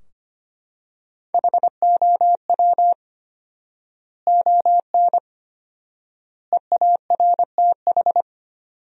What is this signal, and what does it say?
Morse code, heard as a keyed tone, about 25 wpm, 700 Hz. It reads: HOW ON EARTH
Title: 4x Word-Spacing - Sentences from Top 200 Words - 25wpm